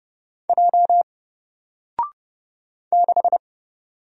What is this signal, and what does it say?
Morse code, heard as a keyed tone, about 30 wpm, 700 Hz. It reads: J 6
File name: Single-Letter-Number-30wpm